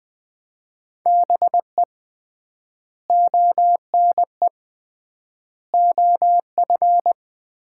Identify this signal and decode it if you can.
Morse code, heard as a keyed tone, about 20 wpm, 700 Hz. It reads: BE ONE OF
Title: 3x Word-Spacing - Sets of 4 Words - 20wpm